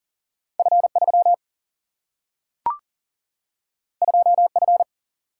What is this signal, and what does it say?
Morse code, heard as a keyed tone, about 40 wpm, 700 Hz. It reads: F 3 2 F
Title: Single-Letter Single-Number 40wpm